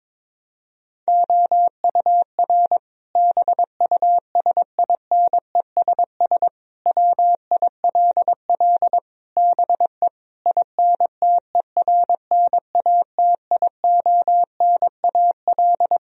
Morse code, heard as a keyed tone, about 22 wpm, 700 Hz. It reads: OUR BUSINESS WILL BE INTERNATIONAL
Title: Sentences from Top 1000 Words - Review Part 4 - 22wpm